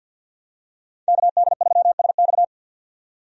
Morse code, heard as a keyed tone, about 52 wpm, 700 Hz. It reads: KD3SX
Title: USA Call Signs - 5 Characters - 50wpm